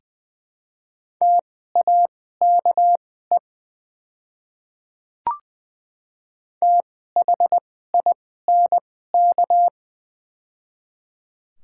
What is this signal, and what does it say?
Morse code, heard as a keyed tone, about 20 wpm, spaced as for 14 wpm, 700 Hz. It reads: TAKE THINK
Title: 2x Character Spacing - Top 200 Words - 20wpm